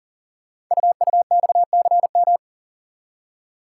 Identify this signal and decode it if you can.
Morse code, heard as a keyed tone, about 38 wpm, 700 Hz. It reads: UUXCM
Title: Code Groups - 5 Letters - 40wpm